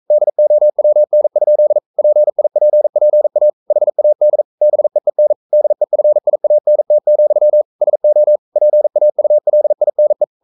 Morse code, heard as a keyed tone, about 42 wpm, 600 Hz. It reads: DOWN? PIPPA HAD BEEN DEFIANT, SO PAULINE
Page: Romanen Never af Ken Follet på engelsk.